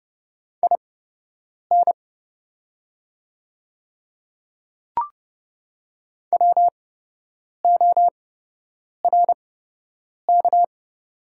Morse code, heard as a keyed tone, about 30 wpm, spaced as for 8 wpm, 700 Hz. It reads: IN WORK
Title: 8x Character Spacing - Top 300 Words - 30wpm